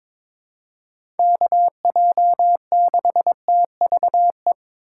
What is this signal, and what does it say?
Morse code, heard as a keyed tone, about 22 wpm, 700 Hz. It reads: KJ6TVE
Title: USA Call Signs - 6 Characters - 3x Character Spacing - 22wpm